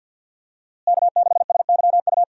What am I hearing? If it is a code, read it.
Morse code, heard as a keyed tone, about 52 wpm, 700 Hz. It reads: K6SYU
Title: USA Call Signs - 5 Characters - Repeated 2x - 50wpm